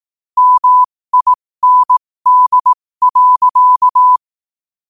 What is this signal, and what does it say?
Morse code, heard as a keyed tone, about 18 wpm, spaced as for 15 wpm, 1000 Hz. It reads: MIND.
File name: qotdfiles / 15wpm